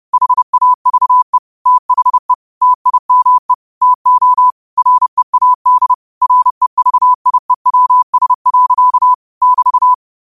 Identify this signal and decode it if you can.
Morse code, heard as a keyed tone, about 30 wpm, 1000 Hz. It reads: HAVE THE TIME TO READ REVIEWS. <BT>
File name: qotdfiles / 30wpm